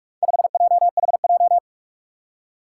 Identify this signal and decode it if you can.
Morse code, heard as a keyed tone, about 42 wpm, 700 Hz. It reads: 5JHJ
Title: Four Challenging Letter-Number Combinations 45wpm